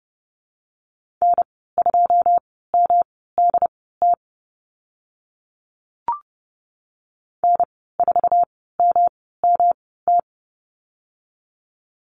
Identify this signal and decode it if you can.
Morse code, heard as a keyed tone, about 30 wpm, spaced as for 17 wpm, 700 Hz. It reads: N2MDT N4MMT
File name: Lesson 19g - Letter w - Call Signs 30wpm